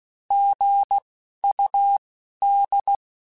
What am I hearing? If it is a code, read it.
Morse code, heard as a keyed tone, about 16 wpm, 800 Hz. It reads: G U D
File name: Abreviations / Abreviations mp3s / Abreviations@10